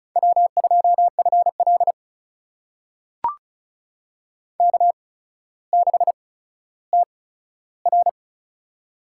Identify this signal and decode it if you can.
Morse code, heard as a keyed tone, about 36 wpm, spaced as for 15 wpm, 700 Hz. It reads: W2FL K6TR
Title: USA Call Signs - 4 Characters - 8x Character Spacing - 35wpm